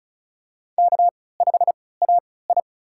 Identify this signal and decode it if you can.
Morse code, heard as a keyed tone, about 34 wpm, spaced as for 20 wpm, 700 Hz. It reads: K5AI
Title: USA Call Signs - 4 Characters - 3x Character Spacing - 35wpm